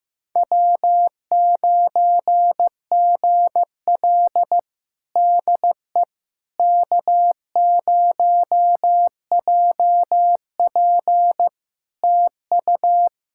Morse code, heard as a keyed tone, about 16 wpm, 700 Hz. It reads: W9GL DE K0JP TU
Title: Practice QSO - Exchange 4 - Ending - 15wpm